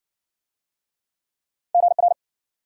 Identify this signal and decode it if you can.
Morse code, heard as a keyed tone, about 56 wpm, 700 Hz. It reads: GR